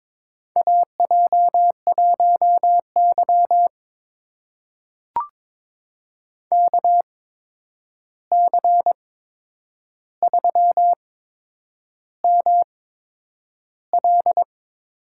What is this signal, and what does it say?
Morse code, heard as a keyed tone, about 22 wpm, spaced as for 9 wpm, 700 Hz. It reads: AJ1Y KC3ML
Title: USA Call Signs - 8x Character Spacing - 22wpm